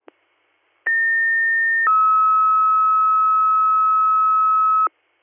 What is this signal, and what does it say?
Two-tone page, heard as a one-second tone, then a three-second tone, about 1800 then 1290 Hz. It is heard from 0.9 to 4.9 s.